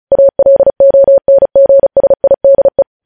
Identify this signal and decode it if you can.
Morse code, heard as a keyed tone, about 36 wpm, 550 Hz. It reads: ALONGSIDE
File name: podcast / News headlines-1x-35x35